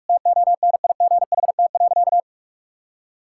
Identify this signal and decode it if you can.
Morse code, heard as a keyed tone, about 46 wpm, 700 Hz. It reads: TONIGHT.
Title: Sentences from Top 1000 Words - Review Part 6 - 45wpm